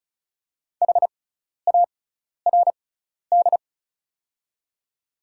Morse code, heard as a keyed tone, about 36 wpm, spaced as for 12 wpm, 700 Hz. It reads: HARD
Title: 6x Character Spacing - Top 500 Words - 35wpm